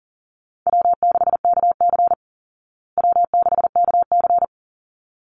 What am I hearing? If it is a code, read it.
Morse code, heard as a keyed tone, about 40 wpm, 700 Hz. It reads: W6KC W6KC
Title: Call Signs Repeated 3x - 40wpm